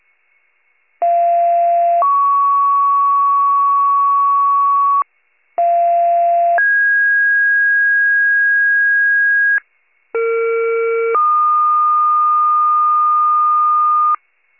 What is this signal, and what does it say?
Two-tone page, heard as a one-second tone, then a three-second tone, about 690 then 1120 Hz, about 690 then 1650 Hz, about 460 then 1220 Hz. It is heard from 1.0 to 14.2 s.
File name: pager / 41 RESCUE / wcalert